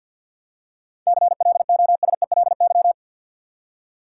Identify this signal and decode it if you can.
Morse code, heard as a keyed tone, about 48 wpm, 700 Hz. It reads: CLOSELY
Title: Seven Letter Words - 50wpm